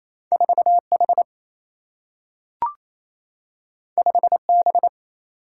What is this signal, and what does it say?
Morse code, heard as a keyed tone, about 28 wpm, 700 Hz. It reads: 4 H 5 B
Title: Single-Letter Single-Number 28wpm